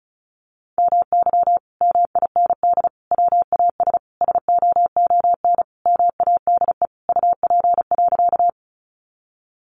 Morse code, heard as a keyed tone, about 34 wpm, 700 Hz. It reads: MY MIND WAS SOON MADE UP.
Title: Sentences from Top 500 Words 35wpm